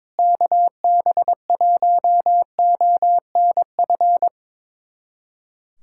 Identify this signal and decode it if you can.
Morse code, heard as a keyed tone, about 22 wpm, 700 Hz. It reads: KB1ONF
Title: USA Callsign DE Callsign - 6 Characters - 22wpm